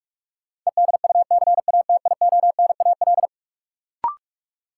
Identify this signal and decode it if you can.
Morse code, heard as a keyed tone, about 42 wpm, 700 Hz. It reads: EDUCATIONAL
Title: Sentences from Top 1200 Words - New Word Drill Alt - 45wpm